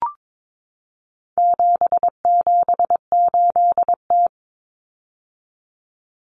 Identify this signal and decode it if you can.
Morse code, heard as a keyed tone, about 22 wpm, 700 Hz. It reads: 778T
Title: Common Cut Numbers - 4-Digit 22wpm